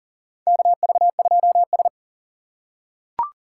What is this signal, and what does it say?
Morse code, heard as a keyed tone, about 40 wpm, 700 Hz. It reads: KV2S
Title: USA Call Signs - 2x Character Spacing - 40wpm